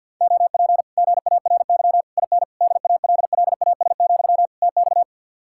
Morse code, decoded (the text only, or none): OP GARY IN DALLAS, TX